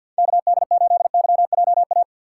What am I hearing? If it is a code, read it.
Morse code, heard as a keyed tone, about 53 wpm, 700 Hz. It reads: K D 8 Y J A
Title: USA Call Signs - 6 Characters - 6x Character Spacing - 50wpm